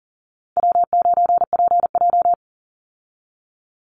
Morse code, heard as a keyed tone, about 40 wpm, 700 Hz. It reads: W9PJ
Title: Easy Call Signs - 40wpm